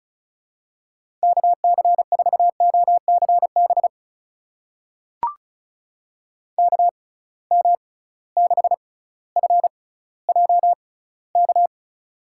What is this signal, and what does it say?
Morse code, heard as a keyed tone, about 36 wpm, spaced as for 18 wpm, 700 Hz. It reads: KC4OCB KM6FJK
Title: USA Call Signs - 6 Characters - 6x Character Spacing - 35wpm